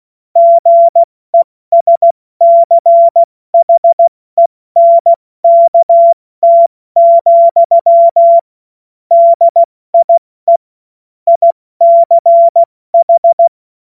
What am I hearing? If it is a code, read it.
Morse code, heard as a keyed tone, about 16 wpm, 680 Hz. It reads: GESCHENKT, DIE ICH
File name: Morsekurs / Witze / Witz 10. Januar 2023 70 F80